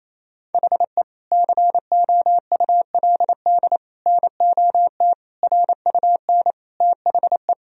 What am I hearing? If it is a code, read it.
Morse code, heard as a keyed tone, about 28 wpm, 700 Hz. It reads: HE COULD NOT RUN THE
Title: Sentences from Top 1000 Words - Review Part 2 - 28wpm